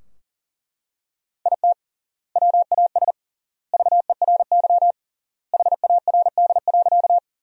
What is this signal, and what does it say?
Morse code, heard as a keyed tone, about 40 wpm, 700 Hz. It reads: IT WAS VERY HARD.
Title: 3x Word-Spacing - Sentences from Top 300 Words - 40wpm